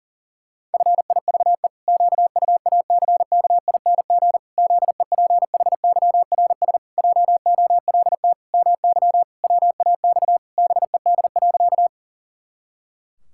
Morse code, heard as a keyed tone, about 38 wpm, 700 Hz. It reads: FIVE QUACKING ZEPHYRS JOLT MY WAX BED.
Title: Pangram Challenge v2 - 40wpm